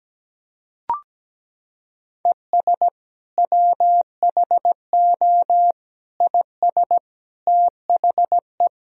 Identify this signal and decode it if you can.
Morse code, heard as a keyed tone, about 17 wpm, 700 Hz. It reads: ES WHO IS THE
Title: Sentences from Top 200 Words and Abbreviations 17wpm